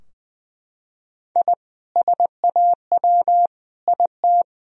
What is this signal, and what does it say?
Morse code, heard as a keyed tone, about 20 wpm, 700 Hz. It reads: I SAW IT
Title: Sentences from Top 1000 Words - Review Part 2 - 20wpm